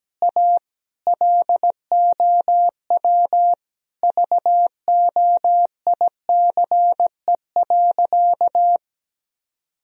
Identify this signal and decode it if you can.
Morse code, heard as a keyed tone, about 17 wpm, 700 Hz. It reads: A LOW VOICE.
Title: Sentences from Top 1000 Words - Review Part 1 - 17wpm